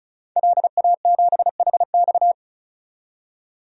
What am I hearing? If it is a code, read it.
Morse code, heard as a keyed tone, about 34 wpm, 700 Hz. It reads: LA7HX
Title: DXCC-100 Call Signs - 5 Characters - 35wpm